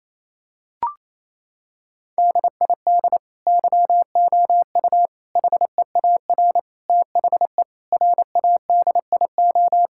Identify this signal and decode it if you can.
Morse code, heard as a keyed tone, about 28 wpm, 700 Hz. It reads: DID YOU HEAR THE RADIO
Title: Sentences from Top 1000 Words - Review Part 5 - 28wpm